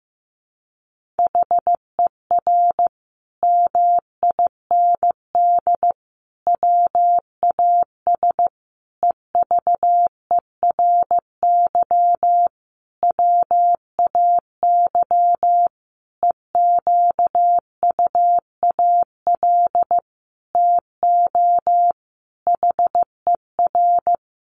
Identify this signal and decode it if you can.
Morse code, heard as a keyed tone, about 15 wpm, 700 Hz. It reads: HER MIND WAS EVERY WAY EQUAL TO HER
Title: Sentences from Top 1400 Words 15wpm